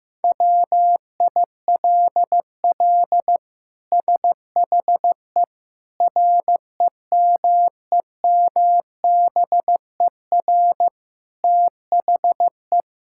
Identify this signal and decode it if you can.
Morse code, heard as a keyed tone, about 16 wpm, 700 Hz. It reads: WILL SHE REMEMBER THE
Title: Sentences from Top 1000 Words - Review Part 6 - 15wpm